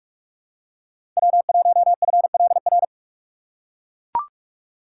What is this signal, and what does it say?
Morse code, heard as a keyed tone, about 44 wpm, 690 Hz. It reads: W1FLR
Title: USA Call Signs - 5 Characters - 6x Character Spacing - 45wpm